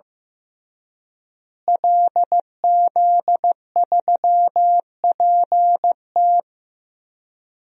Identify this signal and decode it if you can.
Morse code, heard as a keyed tone, about 15 wpm, 700 Hz. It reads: LZ3PT
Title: DXCC-100 Call Signs - 5 Characters - Repeated 3x - 15wpm